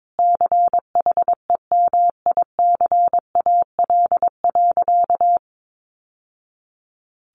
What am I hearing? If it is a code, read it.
Morse code, heard as a keyed tone, about 22 wpm, 700 Hz. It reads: CHEMICAL.
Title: Sentences from Top 1200 Words 22wpm